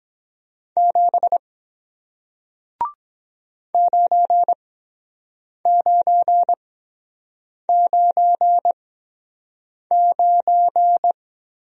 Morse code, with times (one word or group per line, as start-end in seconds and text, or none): 0.8-1.4 7
3.7-11.1 9999